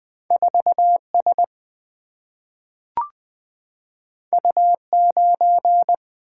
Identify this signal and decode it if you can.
Morse code, heard as a keyed tone, about 20 wpm, 700 Hz. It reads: 4 S U 9
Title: Single-Letter Single-Number 20wpm